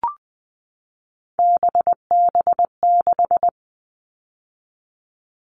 Morse code, heard as a keyed tone, about 20 wpm, 700 Hz. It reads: BB6
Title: All Cut Numbers - 3-Digit 20wpm